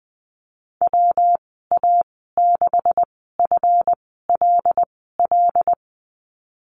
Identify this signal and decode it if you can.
Morse code, heard as a keyed tone, about 19 wpm, spaced as for 14 wpm, 700 Hz. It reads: WA6FLL
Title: 161 Lesson 26g - Letter g - Call Signs 20wpm